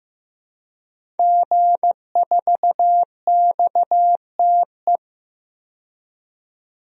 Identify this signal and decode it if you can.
Morse code, heard as a keyed tone, about 15 wpm, 700 Hz. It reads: G4XTE
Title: DXCC-100 Call Signs - 5 Characters - 15wpm